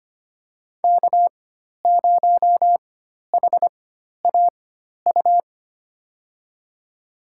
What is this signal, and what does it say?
Morse code, heard as a keyed tone, about 25 wpm, spaced as for 12 wpm, 700 Hz. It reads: K0HAU
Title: USA Call Signs - 5 Characters - 4x Character Spacing - 25wpm